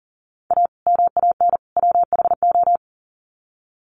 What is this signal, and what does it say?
Morse code, heard as a keyed tone, about 40 wpm, 700 Hz. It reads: A MAN WHO
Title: Sets of 3 Words 40wpm